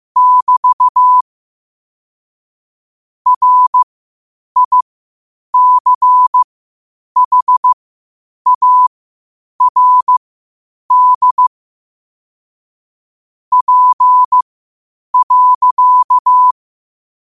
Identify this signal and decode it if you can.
Morse code, heard as a keyed tone, about 15 wpm, spaced as for 8 wpm, 1000 Hz. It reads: <BT> RICHARD P.
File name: 8wpm